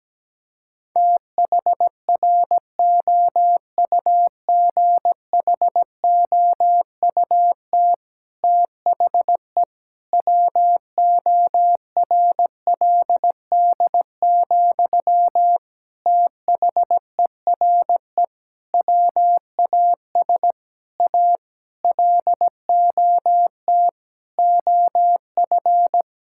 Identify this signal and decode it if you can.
Morse code, heard as a keyed tone, about 17 wpm, 700 Hz. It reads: THROUGHOUT THE WORLD, THERE WAS A LOT OF
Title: Sentences from Top 1000 Words - Review Part 5 - 17wpm